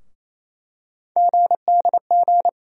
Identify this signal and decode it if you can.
Morse code, heard as a keyed tone, about 28 wpm, 700 Hz. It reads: GDG
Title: USA Call Sign Postfixes - 3 Letters 28wpm